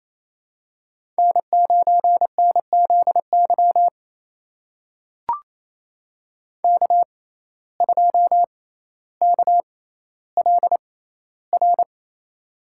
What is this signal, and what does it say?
Morse code, heard as a keyed tone, about 28 wpm, spaced as for 14 wpm, 700 Hz. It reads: N9NZY K2KLR
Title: USA Call Signs - 5 Characters - 6x Character Spacing - 28wpm